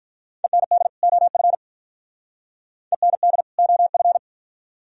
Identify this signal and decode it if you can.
Morse code, heard as a keyed tone, about 47 wpm, 700 Hz. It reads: END OF END OF